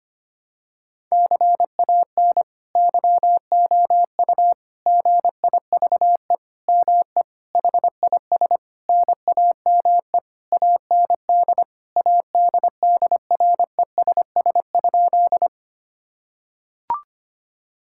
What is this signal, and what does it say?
Morse code, heard as a keyed tone, about 25 wpm, 700 Hz. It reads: CAN YOU GIVE ME HIS NAME AND ADDRESS?
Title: Sentences from Top 1000 Words - Review Part 6 - 25wpm